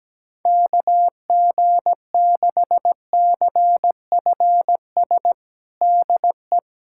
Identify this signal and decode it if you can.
Morse code, heard as a keyed tone, about 17 wpm, 700 Hz. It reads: KG6CFS DE
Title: USA Callsign DE Callsign - 6 Characters - 17wpm